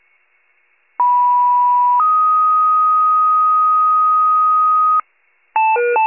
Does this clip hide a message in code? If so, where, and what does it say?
Yes: two-tone page from 1.0 to 5.0 s, heard as a one-second tone, then a three-second tone, about 980 then 1250 Hz.